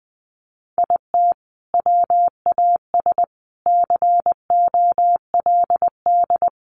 Morse code, heard as a keyed tone, about 20 wpm, 700 Hz. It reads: IT WAS COLD
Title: Sentences from Top 500 Words 20wpm